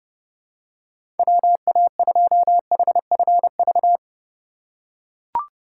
Morse code, heard as a keyed tone, about 30 wpm, 700 Hz. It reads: WA2HFV
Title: USA Call Signs - 2x Character Spacing - 30wpm